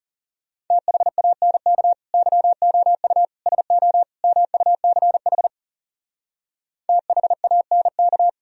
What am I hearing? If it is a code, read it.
Morse code, heard as a keyed tone, about 38 wpm, 700 Hz. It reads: THANK YOU SO MUCH THANK